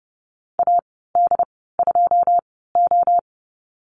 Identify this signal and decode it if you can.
Morse code, heard as a keyed tone, about 30 wpm, spaced as for 17 wpm, 700 Hz. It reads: AD2O
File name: Lesson 14g - Number 2 - Call Signs 30wpm